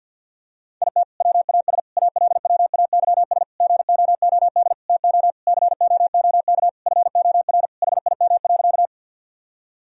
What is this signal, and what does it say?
Morse code, heard as a keyed tone, about 50 wpm, 700 Hz. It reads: IT WAS ALWAYS GOOD TO COOK FOR HIM.
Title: Sentences from Top 1000 Words - Review Part 4 - 50wpm